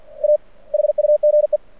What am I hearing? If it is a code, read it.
Morse code, heard as a keyed tone, about 51 wpm, 600 Hz. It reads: A HUGE
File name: archived / CRSnet